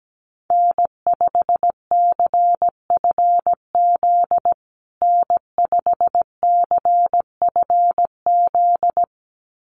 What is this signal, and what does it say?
Morse code, heard as a keyed tone, about 17 wpm, 700 Hz. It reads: N5CFZ N5CFZ
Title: USA Call Signs Repeated 2x 17wpm